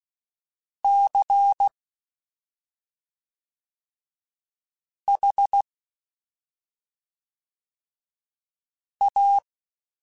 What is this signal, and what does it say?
Morse code, heard as a keyed tone, about 16 wpm, 800 Hz. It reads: C H A
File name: Special INGs / Special INGs-mp3s / SlowToFaster ing @5